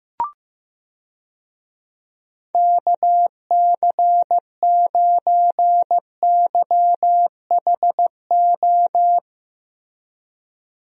Morse code, heard as a keyed tone, about 15 wpm, 700 Hz. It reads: KC9YHO
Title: USA Call Signs - 6 Characters - Repeated 2x - 15wpm